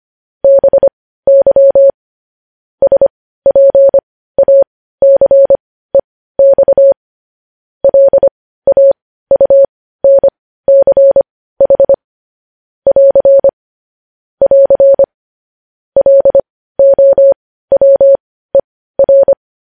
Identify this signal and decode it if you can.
Morse code, heard as a keyed tone, about 25 wpm, spaced as for 15 wpm, 550 Hz. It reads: BY SPACEX LAUNCH <AR> <AR> LOWER